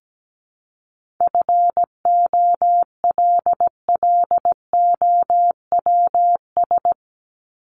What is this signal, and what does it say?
Morse code, heard as a keyed tone, about 17 wpm, 700 Hz. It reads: FOLLOWS
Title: ES:S Suffix - Top 100 Words with Base Words 17wpm